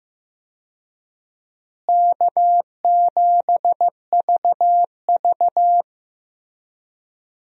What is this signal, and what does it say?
Morse code, heard as a keyed tone, about 15 wpm, 700 Hz. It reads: K7VV
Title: USA Call Signs - 4 Characters - Repeated 3x - 15wpm